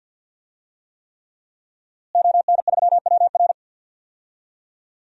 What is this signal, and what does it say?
Morse code, heard as a keyed tone, about 47 wpm, 700 Hz. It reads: ON3WR
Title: DXCC-100 Call Signs - 5 Characters - Repeated 3x - 50wpm